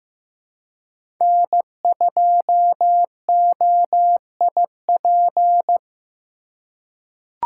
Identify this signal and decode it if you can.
Morse code, heard as a keyed tone, about 15 wpm, 700 Hz. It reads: N2OIP
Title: USA Call Signs - 5 Characters - 4x Character Spacing - 15wpm